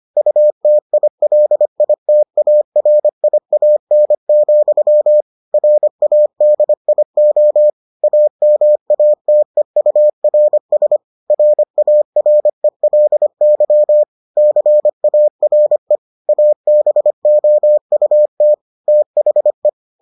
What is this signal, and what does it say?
Morse code, heard as a keyed tone, about 25 wpm, 600 Hz. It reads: UTILITARIAN, RADIO AMATEURS RARELY CARE ABOUT THE